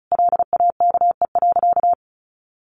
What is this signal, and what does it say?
Morse code, heard as a keyed tone, about 34 wpm, 700 Hz. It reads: LAKE.
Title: Sentences from Top 1100 Words 35wpm